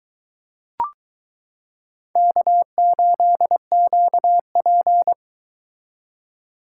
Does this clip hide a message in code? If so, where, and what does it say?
Yes: Morse code from 2.2 to 5.1 s, heard as a keyed tone, about 22 wpm, 700 Hz. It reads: K8QP